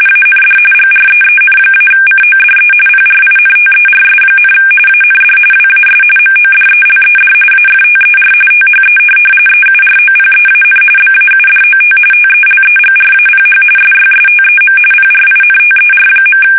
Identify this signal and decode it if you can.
RTTY, about 45.45 baud. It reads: AND GREEN-CODE PRINTERS, AND WAS MODELED AFTER THE EUROPEAN BAUDOT TELEGRAPH SYSTEM PRINTER.
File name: fsk850 / teletype-corp